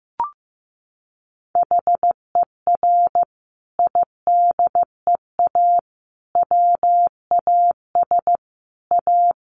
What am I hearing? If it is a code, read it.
Morse code, heard as a keyed tone, about 15 wpm, 700 Hz. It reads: HER IDEA WAS A
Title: Sentences from Top 900 Words 15wpm